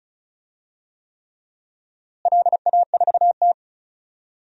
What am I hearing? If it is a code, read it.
Morse code, heard as a keyed tone, about 36 wpm, 690 Hz. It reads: LA4T
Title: DXCC-100 Call Signs - 4 Characters - Repeated 3x - 35wpm